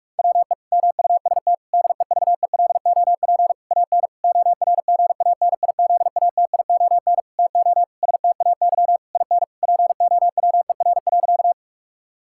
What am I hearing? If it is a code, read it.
Morse code, heard as a keyed tone, about 41 wpm, 700 Hz. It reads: WE MUST DEVELOP AN ORGANIZATION TO STAY IN POWER.
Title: Sentences from Top 1000 Words - Review Part 5 - 45wpm